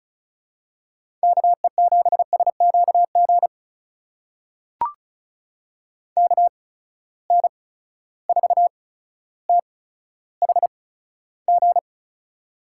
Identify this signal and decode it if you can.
Morse code, heard as a keyed tone, about 36 wpm, spaced as for 15 wpm, 700 Hz. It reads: KE7SQG KN4THG
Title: USA Call Signs - 6 Characters - 8x Character Spacing - 35wpm